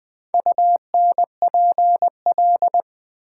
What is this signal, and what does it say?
Morse code, heard as a keyed tone, about 20 wpm, 700 Hz. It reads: UNPL
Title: Build A Word - 4 Letters At A Time - 10-Letter Words - 20wpm